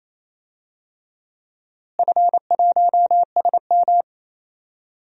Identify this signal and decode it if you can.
Morse code, heard as a keyed tone, about 28 wpm, 700 Hz. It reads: F1SM
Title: DXCC-100 Call Signs - 4 Characters - Repeated 3x - 28wpm